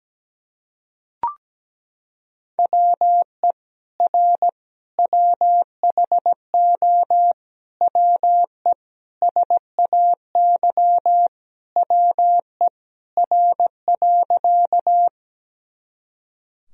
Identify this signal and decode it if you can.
Morse code, heard as a keyed tone, about 17 wpm, 700 Hz. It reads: WE R WHO WE SAY WE R.
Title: Sentences from Top 200 Words and Abbreviations 17wpm